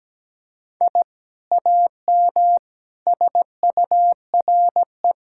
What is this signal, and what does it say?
Morse code, heard as a keyed tone, about 17 wpm, 700 Hz. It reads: I AM SURE
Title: Sentences from Top 1000 Words - Review Part 6 - 17wpm